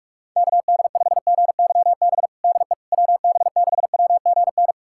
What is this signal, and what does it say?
Morse code, heard as a keyed tone, about 46 wpm, 700 Hz. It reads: KD5GYB DE WB6WGN
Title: USA Callsign DE Callsign - 6 Characters - 45wpm